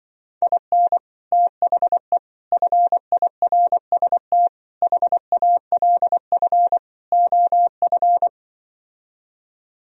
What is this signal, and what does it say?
Morse code, heard as a keyed tone, about 24 wpm, 700 Hz. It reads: IN THE FIRST HALF OF
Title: Speed-Racing - Sets of 5 Words 20wpm